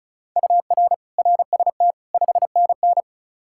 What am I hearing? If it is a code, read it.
Morse code, heard as a keyed tone, about 36 wpm, 700 Hz. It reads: UR RST 5NN
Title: Practice QSO - Exchange 1 - Beginning - 35wpm